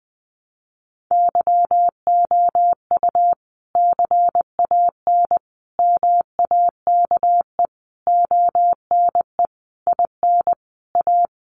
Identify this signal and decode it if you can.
Morse code, heard as a keyed tone, about 20 wpm, 700 Hz. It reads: YOU CAN MAKE ONE IN A
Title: Sentences from Top 1000 Words 20wpm